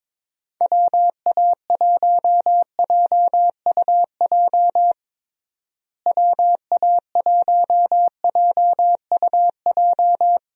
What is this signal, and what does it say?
Morse code, heard as a keyed tone, about 22 wpm, 700 Hz. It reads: WA1JUJ WA1JUJ
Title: USA Call Signs - 6 Characters - Repeated 3x - 22wpm